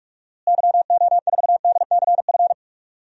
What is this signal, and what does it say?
Morse code, heard as a keyed tone, about 46 wpm, 700 Hz. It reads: YO4DCF
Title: DXCC-100 Call Signs - 6 Characters - Repeated 2x - 45wpm